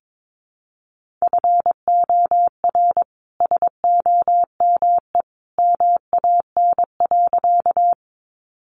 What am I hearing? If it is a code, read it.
Morse code, heard as a keyed tone, about 22 wpm, 700 Hz. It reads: FOR SOME MAN.
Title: Sentences from Top 200 Words 22wpm